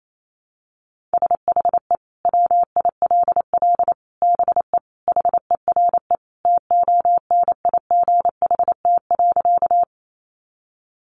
Morse code, heard as a keyed tone, about 28 wpm, 700 Hz. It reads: SHE WILL BE HERE TONIGHT.
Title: Sentences from Top 600 Words 28wpm